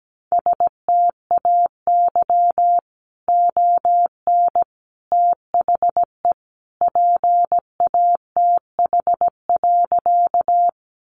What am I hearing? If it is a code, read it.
Morse code, heard as a keyed tone, about 17 wpm, 700 Hz. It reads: STAY ON THE PATH.
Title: A Sound Of Thunder 17wpm